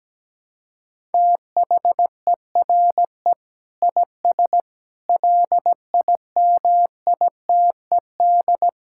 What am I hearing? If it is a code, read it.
Morse code, heard as a keyed tone, about 17 wpm, 700 Hz. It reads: THERE IS LIMITED
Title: Sentences from Top 1000 Words - Review Part 6 - 17wpm